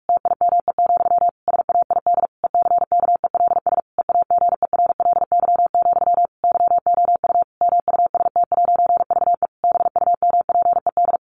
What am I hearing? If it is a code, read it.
Morse code, heard as a keyed tone, about 44 wpm, 700 Hz. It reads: TIME, SAID ECKELS EAGERLY, YOU MUST'VE BUMPED
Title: A Sound Of Thunder 45wpm